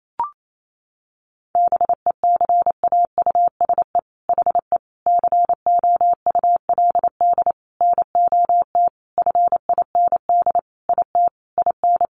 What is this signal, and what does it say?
Morse code, heard as a keyed tone, about 28 wpm, 700 Hz. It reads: BECAUSE HE COULD NOT FIND IT IN
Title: Sentences from Top 1400 Words 28wpm